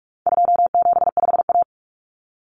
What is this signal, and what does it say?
Morse code, heard as a keyed tone, about 44 wpm, 700 Hz. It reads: 275A
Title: Common Cut Numbers - 4-Digit 45wpm